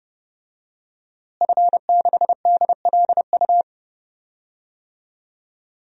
Morse code, heard as a keyed tone, about 32 wpm, 700 Hz. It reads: F6DLU
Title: DXCC-100 Call Signs - 5 Characters - Repeated 2x - 30wpm